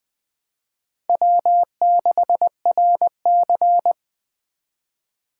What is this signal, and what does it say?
Morse code, heard as a keyed tone, about 21 wpm, 700 Hz. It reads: W6RC
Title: Speed Racing - USA 1x2 Call Signs 17wpm